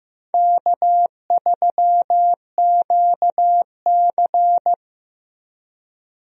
Morse code, heard as a keyed tone, about 15 wpm, 700 Hz. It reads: K3QC
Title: USA Call Signs - 4 Characters - 15wpm